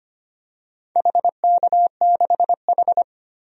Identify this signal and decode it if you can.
Morse code, heard as a keyed tone, about 25 wpm, 690 Hz. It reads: HK6H
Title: DXCC-100 Call Signs - 4 Characters - 25wpm